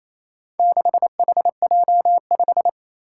Morse code, heard as a keyed tone, about 29 wpm, 700 Hz. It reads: 6HJ5
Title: Four Challenging Letter-Number Combinations 28wpm